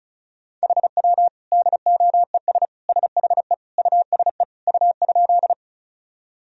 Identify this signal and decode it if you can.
Morse code, heard as a keyed tone, about 36 wpm, 700 Hz. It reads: HW DOES SHE USE U?
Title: Sentences from Top 200 Words and Abbreviations 35wpm